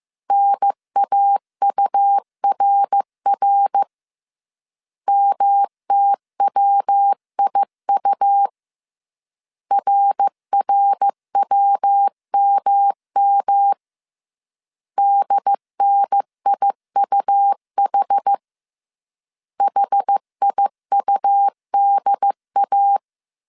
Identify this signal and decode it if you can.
Morse code, heard as a keyed tone, about 15 wpm, 800 Hz. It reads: NAURR MTWIU RRWMM DNIUH HIUDA